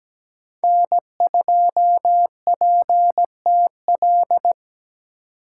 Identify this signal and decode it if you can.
Morse code, heard as a keyed tone, about 17 wpm, 700 Hz. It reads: N2PTL
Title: USA Call Signs - 2x Character Spacing - 17wpm